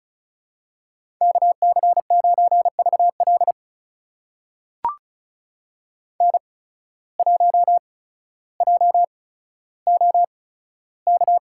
Morse code, heard as a keyed tone, about 36 wpm, spaced as for 15 wpm, 700 Hz. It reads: KC9VL N1JOK
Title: USA Call Signs - 5 Characters - 8x Character Spacing - 35wpm